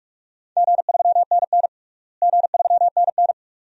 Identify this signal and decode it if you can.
Morse code, heard as a keyed tone, about 43 wpm, 700 Hz. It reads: G3NN G3NN
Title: DXCC-100 Call Signs - 4 Characters - Repeated 2x - 45wpm